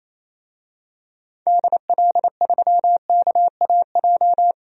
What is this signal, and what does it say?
Morse code, heard as a keyed tone, about 28 wpm, 700 Hz. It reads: DL3KAJ
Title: DXCC-100 Call Signs - 6 Characters - 28wpm